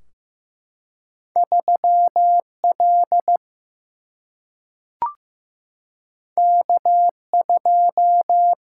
Morse code, heard as a keyed tone, about 15 wpm, 700 Hz. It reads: 3 L K 2
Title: Single-Letter Single-Number 15wpm